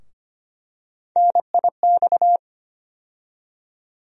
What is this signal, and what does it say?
Morse code, heard as a keyed tone, about 25 wpm, 700 Hz. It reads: NIX
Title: USA Call Sign Postfixes - 3 Letters 25wpm